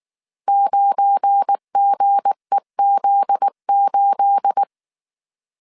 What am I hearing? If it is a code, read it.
Morse code, heard as a keyed tone, about 19 wpm, 800 Hz. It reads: 9GEZ8
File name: CW / DET / 1 9 1500C S C 18wpm 2539